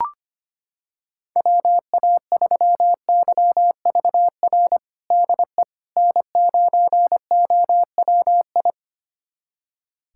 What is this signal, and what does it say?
Morse code, heard as a keyed tone, about 25 wpm, 700 Hz. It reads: WA3YVR DE N9OWI
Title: USA Callsign DE Callsign 25wpm